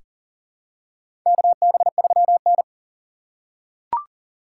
Morse code, heard as a keyed tone, about 40 wpm, 700 Hz. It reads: KB3N
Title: USA Call Signs - 4 Characters - 3x Character Spacing - 40wpm